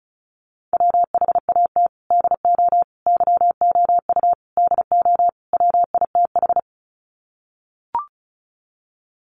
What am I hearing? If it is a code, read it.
Morse code, heard as a keyed tone, about 34 wpm, 700 Hz. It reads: WHAT DO YOU DO WITH
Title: Sets of 5 Words Encore2 35wpm